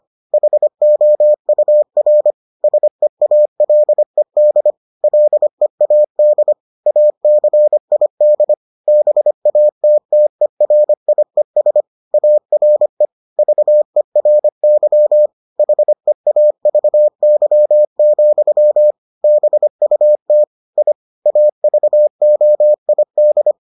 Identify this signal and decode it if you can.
Morse code, heard as a keyed tone, about 25 wpm, 600 Hz. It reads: HOUR SEALED LEAD ACID BATTERIES ARE VERY HEAVY, BUT I AVOID